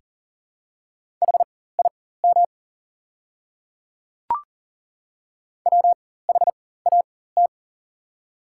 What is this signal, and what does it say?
Morse code, heard as a keyed tone, about 38 wpm, spaced as for 18 wpm, 690 Hz. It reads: HIM WHAT
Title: 4x Character Spacing - Top 100 Words - 40wpm